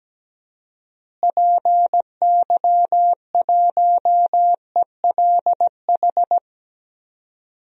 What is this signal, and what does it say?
Morse code, heard as a keyed tone, about 17 wpm, 700 Hz. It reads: PY1ELH
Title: DXCC-100 Call Signs - 6 Characters - Repeated 3x - 17wpm